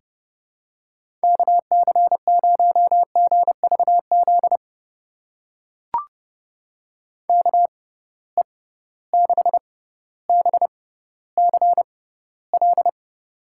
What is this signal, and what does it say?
Morse code, heard as a keyed tone, about 32 wpm, spaced as for 16 wpm, 700 Hz. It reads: KC0GVZ KE6BCL
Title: USA Call Signs - 6 Characters - 6x Character Spacing - 30wpm